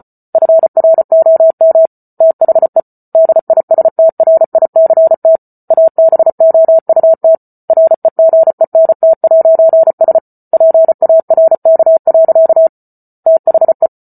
Morse code, heard as a keyed tone, about 34 wpm, 650 Hz. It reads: FROM THE DISTRICT ABOUT REGENT'S PARK. THE